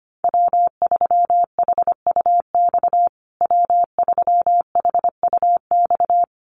Morse code, heard as a keyed tone, about 25 wpm, 700 Hz. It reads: W3HUX W3HUX
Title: USA Call Signs Repeated 2x 25wpm